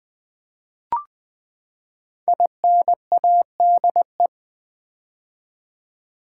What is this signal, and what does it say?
Morse code, heard as a keyed tone, about 20 wpm, 700 Hz. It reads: INADE
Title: Build A Word - 5 Letters At A Time - 10-Letter Words - 20wpm